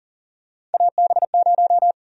Morse code, heard as a keyed tone, about 37 wpm, 700 Hz. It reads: AB0
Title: USA Call Sign Prefixes - 40wpm